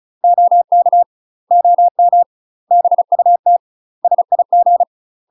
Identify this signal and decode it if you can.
Morse code, heard as a keyed tone, about 36 wpm, 700 Hz. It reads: OK OM BUT SIG
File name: UK MASTER QSOs TRAINING / 35WPM / UK MASTER QSOs 0038